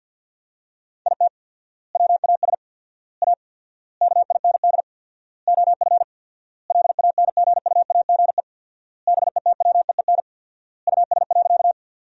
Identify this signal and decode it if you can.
Morse code, heard as a keyed tone, about 51 wpm, 700 Hz. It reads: IT WAS A KIND OF LANGUAGE BETWEEN US.
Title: 4x Word-Spacing - Sentences from Top 500 Words - 50wpm